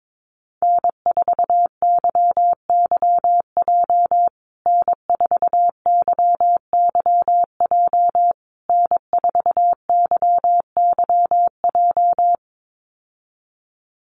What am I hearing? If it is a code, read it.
Morse code, heard as a keyed tone, about 22 wpm, 700 Hz. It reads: N4YYJ N4YYJ N4YYJ
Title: USA Call Signs Repeated 3x 22wpm